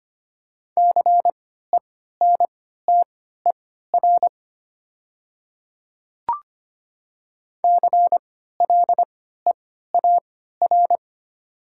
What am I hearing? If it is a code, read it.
Morse code, heard as a keyed tone, about 25 wpm, spaced as for 14 wpm, 690 Hz. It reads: CENTER CLEAR
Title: 3x Character Spacing - Top 500 Words - 25wpm